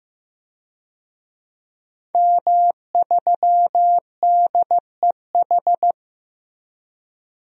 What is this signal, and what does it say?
Morse code, heard as a keyed tone, about 15 wpm, 700 Hz. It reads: M3DEH
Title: DXCC-100 Call Signs - 5 Characters - Repeated 2x - 15wpm